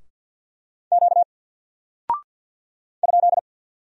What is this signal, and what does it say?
Morse code, heard as a keyed tone, about 48 wpm, 700 Hz. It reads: Q ?
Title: Single Character (Letter:Number:Punctuation:Prosign) Alt 50wpm